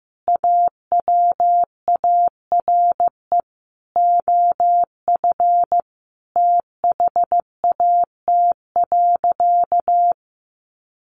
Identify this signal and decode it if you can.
Morse code, heard as a keyed tone, about 15 wpm, 700 Hz. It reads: AWARE OF THAT.
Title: Sentences from Top 1200 Words 15wpm